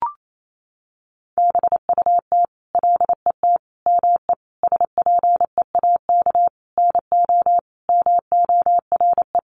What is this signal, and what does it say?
Morse code, heard as a keyed tone, about 28 wpm, 700 Hz. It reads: BUT LET ME SPEAK NO MORE
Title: Sentences from Top 1100 Words 28wpm